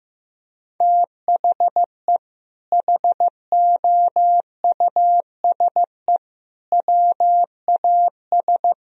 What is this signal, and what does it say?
Morse code, heard as a keyed tone, about 16 wpm, 700 Hz. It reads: THE HOUSE WAS
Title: Sentences from Top 1000 Words - Review Part 3 - 15wpm